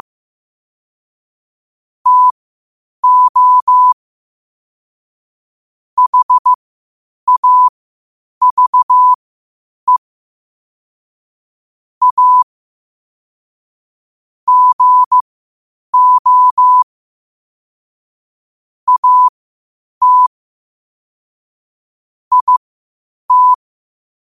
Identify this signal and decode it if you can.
Morse code, heard as a keyed tone, about 15 wpm, spaced as for 8 wpm, 1000 Hz. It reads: TO HAVE A GO AT IT
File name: qotdfiles / 8wpm